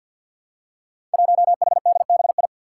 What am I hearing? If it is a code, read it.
Morse code, heard as a keyed tone, about 45 wpm, 700 Hz. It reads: 1HDBI
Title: Five Challenging Letter-Number Combinations 50wpm